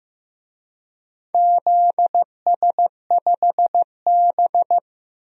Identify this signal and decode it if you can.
Morse code, heard as a keyed tone, about 15 wpm, 700 Hz. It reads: ZS5B
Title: DXCC-100 Call Signs - 4 Characters - Repeated 2x - 15wpm